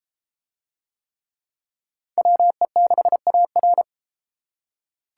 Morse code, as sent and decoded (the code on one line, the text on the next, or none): .-- . -.... .- .-.
WE6AR